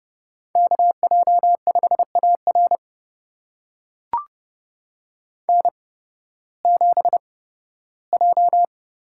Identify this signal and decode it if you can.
Morse code, heard as a keyed tone, about 30 wpm, spaced as for 16 wpm, 700 Hz. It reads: KJ5AR N7J
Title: USA Call Signs - 5 Characters - 8x Character Spacing - 30wpm